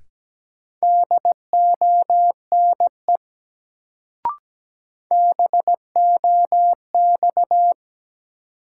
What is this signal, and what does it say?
Morse code, heard as a keyed tone, about 17 wpm, 700 Hz. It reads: DONE BOX
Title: Sentences from Top 500 Words - New Word Drill - 17wpm